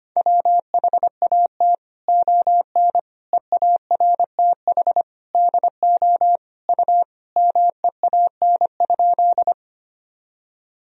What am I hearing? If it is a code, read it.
Morse code, heard as a keyed tone, about 25 wpm, 700 Hz. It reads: WHAT ON EARTH DO U MEAN?
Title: Sentences from Top 200 Words and Abbreviations 25wpm